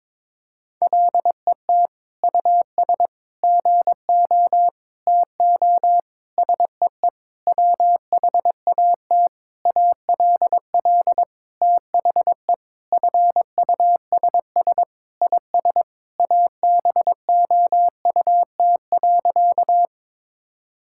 Morse code, heard as a keyed tone, about 22 wpm, 700 Hz. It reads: LET US GO TO SEE WHAT ALL THE FUSS IS ABOUT.
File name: practice / 22wpm / Sentences with 8-Word Phrases - 22wpm